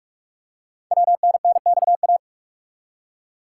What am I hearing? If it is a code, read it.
Morse code, heard as a keyed tone, about 44 wpm, 690 Hz. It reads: WNNXA
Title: Code Groups - 5 Letters - 45wpm